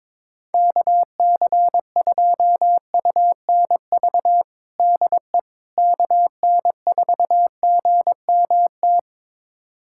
Morse code, heard as a keyed tone, about 22 wpm, 700 Hz. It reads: KC2UNV DE KN4GMT
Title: USA Callsign DE Callsign - 6 Characters - 22wpm